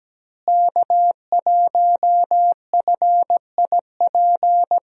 Morse code, heard as a keyed tone, about 17 wpm, 700 Hz. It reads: K1FIP
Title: USA Call Signs - 5 Characters - Repeated 3x - 17wpm